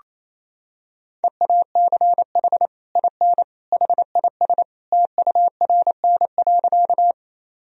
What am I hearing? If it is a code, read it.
Morse code, heard as a keyed tone, about 27 wpm, 700 Hz. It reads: EACH IN HIS TURN.
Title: Sentences from Top 1000 Words - Review Part 1 - 28wpm